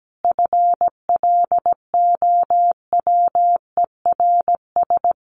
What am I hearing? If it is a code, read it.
Morse code, heard as a keyed tone, about 17 wpm, 700 Hz. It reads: FLOWERS
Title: A Sound Of Thunder 17wpm